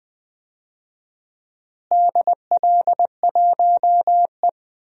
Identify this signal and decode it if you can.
Morse code, heard as a keyed tone, about 20 wpm, 700 Hz. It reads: DL1E
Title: DXCC-100 Call Signs - 4 Characters - Repeated 2x - 20wpm